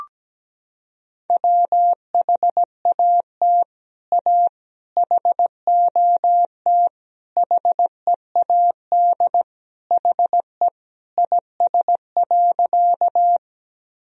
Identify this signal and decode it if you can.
Morse code, heard as a keyed tone, about 17 wpm, 700 Hz. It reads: WHAT A HOT HEAD HE IS.
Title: Sentences from Top 1000 Words - Review Part 3 - 17wpm